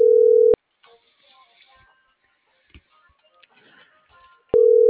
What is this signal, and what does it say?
Ringback tone (North America), 440 + 480 Hz.